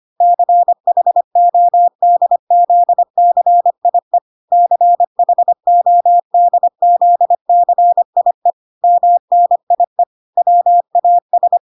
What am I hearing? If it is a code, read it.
Morse code, heard as a keyed tone, about 25 wpm, 700 Hz. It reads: CHODZCIE CHODZCIE MNIE WAS